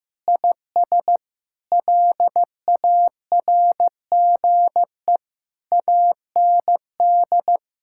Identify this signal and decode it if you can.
Morse code, heard as a keyed tone, about 16 wpm, 700 Hz. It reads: IS LARGE AND
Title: Sentences from Top 1000 Words - Review Part 5 - 15wpm